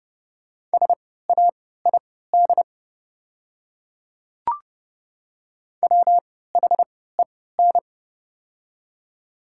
Morse code, heard as a keyed tone, about 30 wpm, spaced as for 17 wpm, 700 Hz. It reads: SAID WHEN
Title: 3x Character Spacing - Top 100 Words - 30wpm